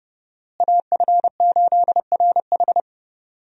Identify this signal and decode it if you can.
Morse code, heard as a keyed tone, about 30 wpm, 700 Hz. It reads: AF8RH
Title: USA Call Signs - 5 Characters - Repeated 2x - 30wpm